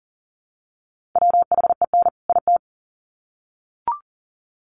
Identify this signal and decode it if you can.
Morse code, heard as a keyed tone, about 40 wpm, 700 Hz. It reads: WHEN IT
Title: Sets of 2 Words 40wpm